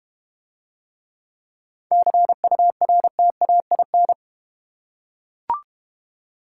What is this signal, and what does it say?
Morse code, heard as a keyed tone, about 31 wpm, 700 Hz. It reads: CURTAIN
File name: Easy Words - Speed-Racing with Hints - 25wpm